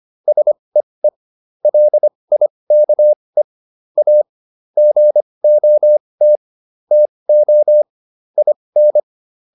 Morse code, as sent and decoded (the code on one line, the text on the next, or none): ... . / .-.. .. -.- . / .- / --. --- - / - --- / .. -.
SEE LIKE A GOT TO IN